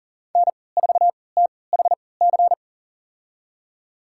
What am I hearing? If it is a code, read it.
Morse code, heard as a keyed tone, about 39 wpm, spaced as for 23 wpm, 700 Hz. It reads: N4THC
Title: USA Call Signs - 5 Characters - 3x Character Spacing - 40wpm